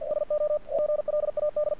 Morse code, heard as a keyed tone, about 48 wpm, 600 Hz. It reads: SO GLAD
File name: archived / CRSnet